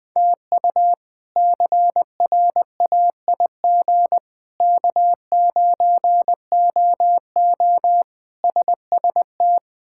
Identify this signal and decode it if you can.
Morse code, heard as a keyed tone, about 21 wpm, 700 Hz. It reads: TU CRAIG K9OO SST
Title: K1USN SST QSO Practice - 20wpm